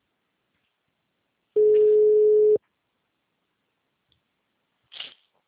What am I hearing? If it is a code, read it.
Ringback tone (Europe, 425 Hz).
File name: out- -2007- - - .8520